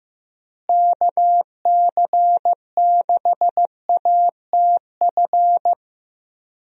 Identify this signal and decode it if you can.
Morse code, heard as a keyed tone, about 15 wpm, 700 Hz. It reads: KC6ATF
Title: USA Call Signs - 6 Characters - Repeated 3x - 15wpm